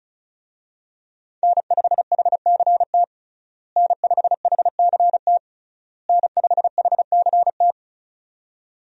Morse code, heard as a keyed tone, about 34 wpm, 700 Hz. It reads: N5HCT N5HCT N5HCT
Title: USA Call Signs - 5 Characters - Repeated 3x - 35wpm